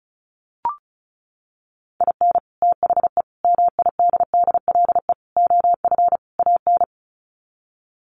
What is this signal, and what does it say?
Morse code, heard as a keyed tone, about 34 wpm, 700 Hz. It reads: IN THE MIDDLE OF AN
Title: Sets of 5 Words Encore2 35wpm